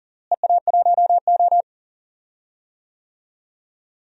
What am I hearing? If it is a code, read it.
Morse code, heard as a keyed tone, about 37 wpm, 700 Hz. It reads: EA1O
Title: DXCC-100 Call Signs - 4 Characters - Repeated 2x - 40wpm